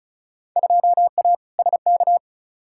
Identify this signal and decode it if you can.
Morse code, heard as a keyed tone, about 36 wpm, 700 Hz. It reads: 2A SK
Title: Field Day Exchanges Encore - 35wpm